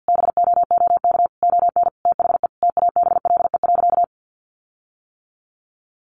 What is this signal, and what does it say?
Morse code, heard as a keyed tone, about 50 wpm, 700 Hz. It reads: BOOK ON THE TABLE.
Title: Sentences from Top 1400 Words 50wpm